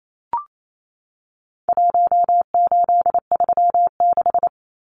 Morse code, heard as a keyed tone, about 28 wpm, 700 Hz. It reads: 1836
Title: Numbers - Contest Style - 4-Digit 28wpm